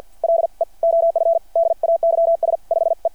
Morse code, heard as a keyed tone, about 50 wpm, 690 Hz. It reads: LE OF DAYS 5E
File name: archived / CRSnet